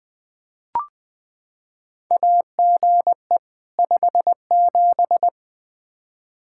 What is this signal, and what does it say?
Morse code, heard as a keyed tone, about 20 wpm, 700 Hz. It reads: AGE 57
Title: Practice QSO - Exchange 3 - More Info - 20wpm